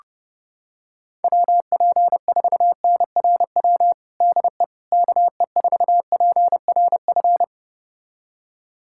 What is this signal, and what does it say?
Morse code, heard as a keyed tone, about 32 wpm, 700 Hz. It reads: WP4NRW DE KE4PRF
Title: USA Callsign DE Callsign - 6 Characters - 30wpm